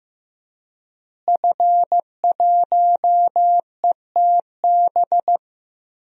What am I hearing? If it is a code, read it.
Morse code, heard as a keyed tone, about 15 wpm, 700 Hz. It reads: F1ETB
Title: DXCC-100 Call Signs - 5 Characters - Repeated 3x - 15wpm